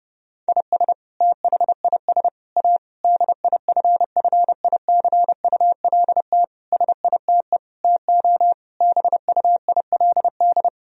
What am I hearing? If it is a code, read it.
Morse code, heard as a keyed tone, about 30 wpm, 700 Hz. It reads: IS THIS A DIFFICULT SITE TO BUILD
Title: Sentences from Top 1000 Words - Review Part 4 - 30wpm